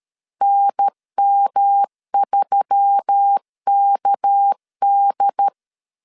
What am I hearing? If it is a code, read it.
Morse code, heard as a keyed tone, about 13 wpm, 800 Hz. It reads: NM3KD
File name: CW / DET / 1 6 1500C S A 12wpm 3440